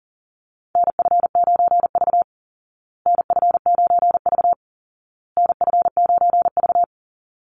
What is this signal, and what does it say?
Morse code, heard as a keyed tone, about 40 wpm, 700 Hz. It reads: NF9V NF9V NF9V
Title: Call Signs Repeated 2x - 40wpm